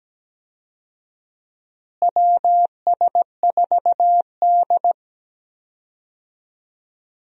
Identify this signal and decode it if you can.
Morse code, heard as a keyed tone, about 17 wpm, 700 Hz. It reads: WS4D
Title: USA Call Signs - 4 Characters - Repeated 2x - 17wpm